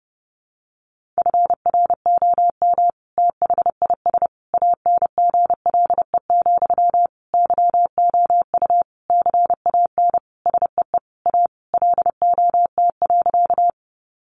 Morse code, heard as a keyed tone, about 30 wpm, 700 Hz. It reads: FROM THIS ANGLE, YOU CAN SEE A LOT.
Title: Sentences from Top 1500 Words 30wpm